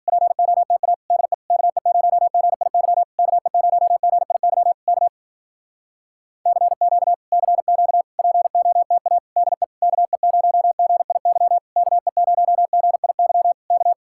Morse code, heard as a keyed tone, about 52 wpm, 700 Hz. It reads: POTA DE KE0GIY KE0GIY K CQ CQ POTA DE KE0GIY KE0GIY K